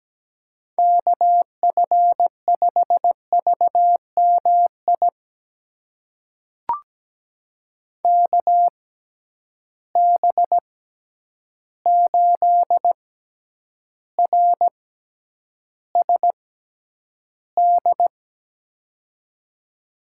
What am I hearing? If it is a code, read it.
Morse code, heard as a keyed tone, about 17 wpm, spaced as for 9 wpm, 700 Hz. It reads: KF5VMI KB8RSD
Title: USA Call Signs - 6 Characters - 6x Character Spacing - 17wpm